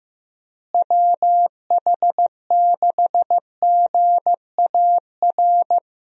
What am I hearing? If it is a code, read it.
Morse code, heard as a keyed tone, about 15 wpm, 700 Hz. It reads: WH6GAR
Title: USA Call Signs - 6 Characters - Repeated 3x - 15wpm